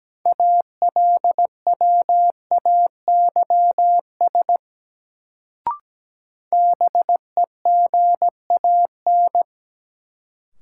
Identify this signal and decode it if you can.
Morse code, heard as a keyed tone, about 17 wpm, 700 Hz. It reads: ALWAYS BEGAN
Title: Sentences from Top 300 Words - New Word Drill - 17wpm